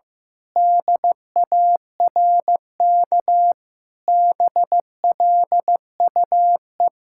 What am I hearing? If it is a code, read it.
Morse code, heard as a keyed tone, about 16 wpm, 700 Hz. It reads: DARK BLUE
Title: Sentences from Top 1600 Words 15wpm